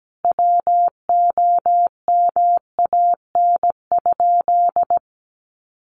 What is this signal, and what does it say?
Morse code, heard as a keyed tone, about 17 wpm, 700 Hz. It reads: WOMAN?
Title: Sentences from Top 1100 Words 17wpm